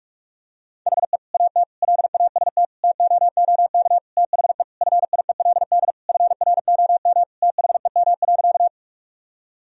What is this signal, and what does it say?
Morse code, heard as a keyed tone, about 46 wpm, 700 Hz. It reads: HE AT LAST TOOK THE FIELD FROM THEM.
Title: Sentences from Top 1000 Words - Review Part 2 - 45wpm